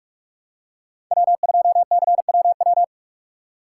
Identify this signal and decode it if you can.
Morse code, heard as a keyed tone, about 46 wpm, 700 Hz. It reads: W2CWW
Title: USA Call Signs - 5 Characters - 4x Character Spacing - 45wpm